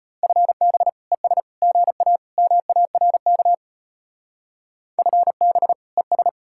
Morse code, heard as a keyed tone, about 33 wpm, 700 Hz. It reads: FB ES GA MARK FB ES